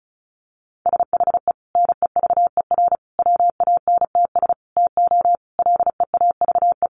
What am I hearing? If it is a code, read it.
Morse code, heard as a keyed tone, about 34 wpm, 700 Hz. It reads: SHE NEVER WANTS TO LEAVE
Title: Easy Phrases 35wpm